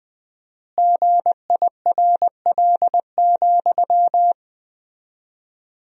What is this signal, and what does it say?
Morse code, heard as a keyed tone, about 20 wpm, 700 Hz. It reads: GIRL,
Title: 4x Word-Spacing - Sentences from Top 400 Words - 20wpm